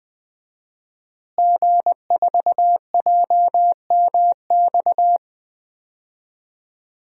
Morse code, heard as a keyed tone, about 20 wpm, 700 Hz. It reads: G4JMX
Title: DXCC-100 Call Signs - 5 Characters - Repeated 3x - 20wpm